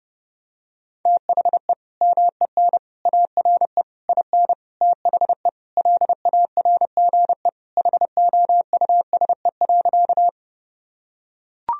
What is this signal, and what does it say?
Morse code, heard as a keyed tone, about 30 wpm, 700 Hz. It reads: THE MEN ARE IN THE LARGE HOUSE.
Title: Sentences from Top 1000 Words - Review Part 1 - 30wpm